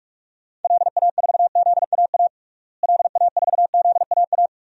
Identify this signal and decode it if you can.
Morse code, heard as a keyed tone, about 46 wpm, 700 Hz. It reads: LA4ZAA LA4ZAA
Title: DXCC-100 Call Signs - 6 Characters - Repeated 2x - 45wpm